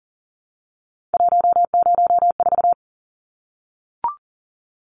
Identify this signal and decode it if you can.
Morse code, heard as a keyed tone, about 40 wpm, 700 Hz. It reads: 104
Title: Numbers - 3-Digit 40wpm